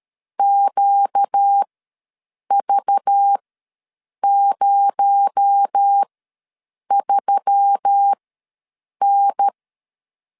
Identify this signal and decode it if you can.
Morse code, heard as a keyed tone, about 13 wpm, spaced as for 7 wpm, 800 Hz. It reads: QV03N
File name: LEZIONI / 7 GF9Z / 7 GF ws 12wpm 3140